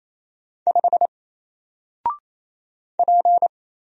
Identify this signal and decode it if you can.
Morse code, heard as a keyed tone, about 28 wpm, 700 Hz. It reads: 5 P
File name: Single-Letter-Number-28wpm